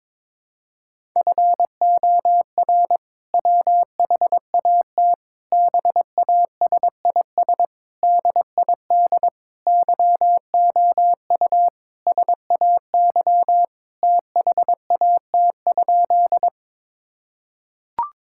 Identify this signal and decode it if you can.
Morse code, heard as a keyed tone, about 22 wpm, 700 Hz. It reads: FOR WHAT BASIS DID YOU SAY THAT?
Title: Sentences from Top 1000 Words - Review Part 5 - 22wpm